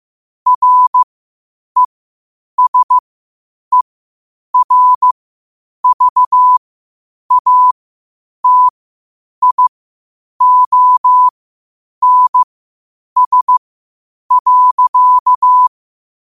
Morse code, heard as a keyed tone, about 15 wpm, spaced as for 8 wpm, 1000 Hz. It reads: RESERVATIONS.
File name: qotdfiles / 8wpm